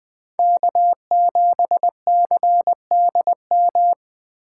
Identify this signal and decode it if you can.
Morse code, heard as a keyed tone, about 20 wpm, 700 Hz. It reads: K7CDM
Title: Speed Racing - USA 1x3 Call Signs 17wpm